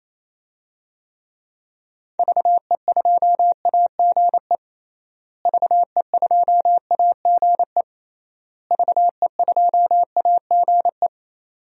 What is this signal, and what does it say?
Morse code, heard as a keyed tone, about 28 wpm, 700 Hz. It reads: VE2AGE VE2AGE VE2AGE
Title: DXCC-100 Call Signs - 6 Characters - Repeated 3x - 28wpm